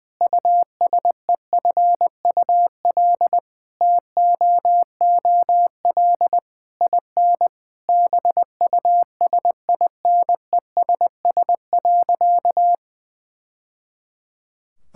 Morse code, heard as a keyed tone, about 21 wpm, 700 Hz. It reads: USEFUL TOOL IN BUSINESS.
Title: Sentences from Top 1000 Words - Review Part 6 - 20wpm